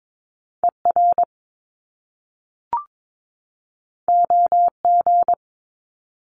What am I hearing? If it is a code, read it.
Morse code, heard as a keyed tone, about 22 wpm, 700 Hz. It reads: ER OG
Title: Top Two Letter Combinations in Words 22wpm